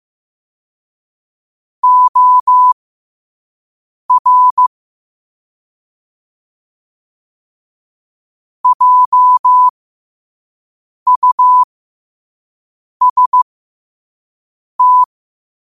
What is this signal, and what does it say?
Morse code, heard as a keyed tone, about 15 wpm, spaced as for 5 wpm, 1000 Hz. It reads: OR JUST
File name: qotdfiles / 5wpm